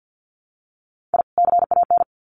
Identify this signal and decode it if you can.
Morse code, heard as a keyed tone, about 50 wpm, 700 Hz. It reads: I CAN
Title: Sets of 2 Words 50wpm